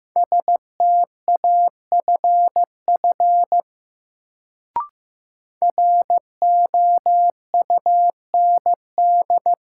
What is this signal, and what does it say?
Morse code, heard as a keyed tone, about 15 wpm, 700 Hz. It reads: STAFF ROUND
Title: Sets of 5 Words Encore - New Word Drill - 15wpm